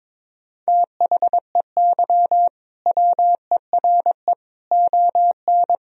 Morse code, heard as a keyed tone, about 21 wpm, 700 Hz. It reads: THEY WERE ON
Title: Sentences from Top 1000 Words - Review Part 3 - 22wpm